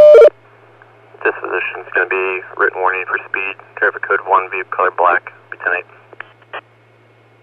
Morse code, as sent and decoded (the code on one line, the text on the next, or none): none